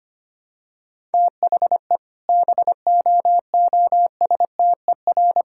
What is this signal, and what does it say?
Morse code, heard as a keyed tone, about 25 wpm, 700 Hz. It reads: THE BOOSTER
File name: practice / 25wpm / All-Age Appropriate Jokes - 25wpm